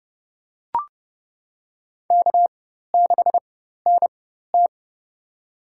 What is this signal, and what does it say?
Morse code, heard as a keyed tone, about 31 wpm, spaced as for 14 wpm, 700 Hz. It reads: K6NT
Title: USA Call Signs - 4 Characters - 4x Character Spacing - 30wpm